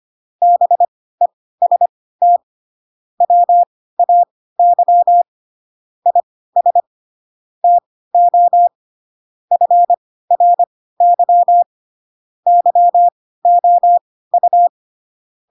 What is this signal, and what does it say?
Morse code, heard as a keyed tone, about 25 wpm, spaced as for 16 wpm, 700 Hz. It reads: BEST WAY IS TO FRY YOU
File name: Pinocchio / Pinocchio ch5 10 25 ik1hge